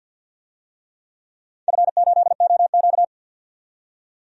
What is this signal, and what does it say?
Morse code, heard as a keyed tone, about 51 wpm, 700 Hz. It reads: F8OQ
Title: DXCC-100 Call Signs - 4 Characters - 50wpm